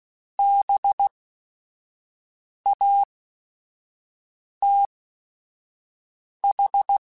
Morse code, heard as a keyed tone, about 16 wpm, spaced as for 5 wpm, 800 Hz. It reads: BATH